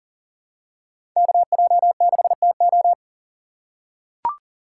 Morse code, heard as a keyed tone, about 40 wpm, 700 Hz. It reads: KJ6TO
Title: USA Call Signs - 5 Characters - 4x Character Spacing - 40wpm